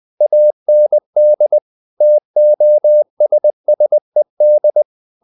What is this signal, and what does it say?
Morse code, heard as a keyed tone, about 20 wpm, 600 Hz. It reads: AND TOSSED
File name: the old man and the sea part 2 20wpm